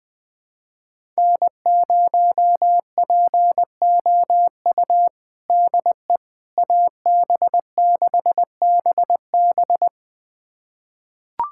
Morse code, heard as a keyed tone, about 21 wpm, 700 Hz. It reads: N0POU DE AB6BB
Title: USA Callsign DE Callsign - 5 Characters - 20wpm